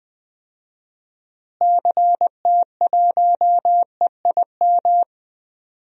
Morse code, heard as a keyed tone, about 20 wpm, 700 Hz. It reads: CT1EIM
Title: DXCC-100 Call Signs - 6 Characters - Repeated 3x - 20wpm